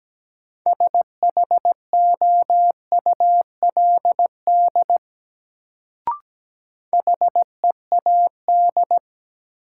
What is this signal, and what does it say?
Morse code, heard as a keyed tone, about 17 wpm, 700 Hz. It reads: SHOULD HEAD
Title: Sentences from Top 300 Words - New Word Drill - 17wpm